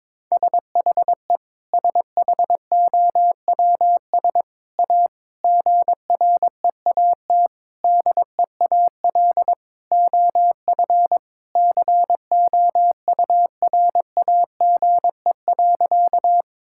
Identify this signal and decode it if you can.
Morse code, heard as a keyed tone, about 22 wpm, 700 Hz. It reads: SHE SHOWS A GREAT DEAL OF COURAGE.
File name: practice / 22wpm / Sentences with 4-Word Phrases - 22wpm